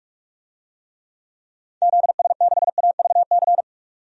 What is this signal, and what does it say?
Morse code, heard as a keyed tone, about 41 wpm, 700 Hz. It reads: ZS6AVC
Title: DXCC-100 Call Signs - 6 Characters - Repeated 3x - 45wpm